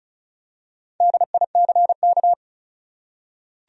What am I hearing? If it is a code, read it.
Morse code, heard as a keyed tone, about 35 wpm, 700 Hz. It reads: DICK
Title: USA Common Names - Max 5 Letters - 35wpm